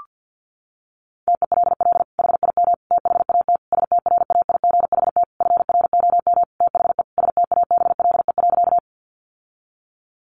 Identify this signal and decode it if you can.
Morse code, heard as a keyed tone, about 50 wpm, 700 Hz. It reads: TELL HIM THAT STRAIGHT FROM THE STABLE.
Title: Sentences from Top 1500 Words 50wpm